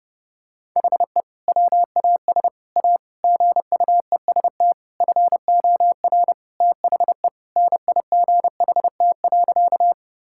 Morse code, heard as a keyed tone, about 30 wpm, 700 Hz. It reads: HE WAS A GUEST FOR THE NIGHT.
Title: Sentences from Top 1600 Words 30wpm